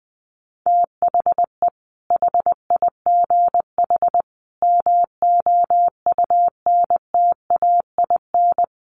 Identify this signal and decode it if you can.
Morse code, heard as a keyed tone, about 20 wpm, 700 Hz. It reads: THE HIGH MOUNTAIN
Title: Sentences from Top 1300 Words 20wpm